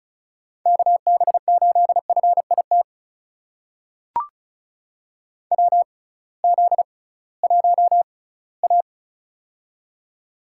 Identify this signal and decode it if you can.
Morse code, heard as a keyed tone, about 36 wpm, spaced as for 20 wpm, 700 Hz. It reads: KB8FIT WZ1A
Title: USA Call Signs - 6x Character Spacing - 35wpm